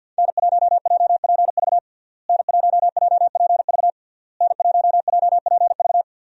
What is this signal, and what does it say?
Morse code, heard as a keyed tone, about 52 wpm, 700 Hz. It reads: N1JPV N1JPV N1JPV
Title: USA Call Signs - 5 Characters - Repeated 3x - 50wpm